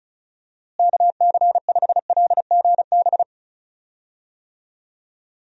Morse code, heard as a keyed tone, about 36 wpm, 700 Hz. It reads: KC5LGB
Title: USA Call Signs - 6 Characters - Repeated 3x - 35wpm